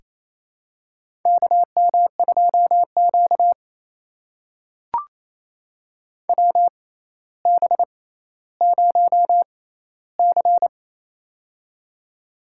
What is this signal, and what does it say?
Morse code, heard as a keyed tone, about 28 wpm, spaced as for 14 wpm, 700 Hz. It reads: KM2Q WB0C
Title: USA Call Signs - 4 Characters - 6x Character Spacing - 28wpm